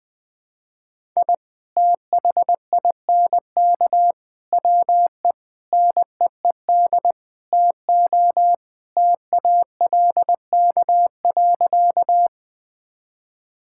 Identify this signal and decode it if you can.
Morse code, heard as a keyed tone, about 20 wpm, 700 Hz. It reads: I THINK WE NEED TO TALK.
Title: Sentences with 5-Word Phrases - 20wpm